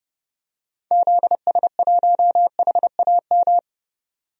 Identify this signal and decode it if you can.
Morse code, heard as a keyed tone, about 30 wpm, 700 Hz. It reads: ZS1HAM
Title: DXCC-100 Call Signs - 6 Characters - 30wpm